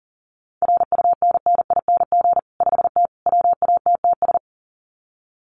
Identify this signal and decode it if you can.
Morse code, heard as a keyed tone, about 40 wpm, 700 Hz. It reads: RUNNING 5T WATTS
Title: Common QSO Phrases 40wpm